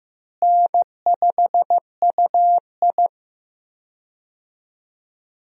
Morse code, heard as a keyed tone, about 15 wpm, 700 Hz. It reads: N5UI
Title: USA Call Signs - 4 Characters - Repeated 3x - 15wpm